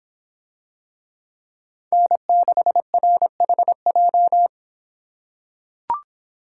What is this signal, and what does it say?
Morse code, heard as a keyed tone, about 24 wpm, 700 Hz. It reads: N6RHJ